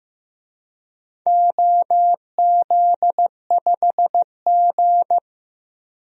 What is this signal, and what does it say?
Morse code, heard as a keyed tone, about 15 wpm, 700 Hz. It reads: OZ5G
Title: DXCC-100 Call Signs - 4 Characters - 15wpm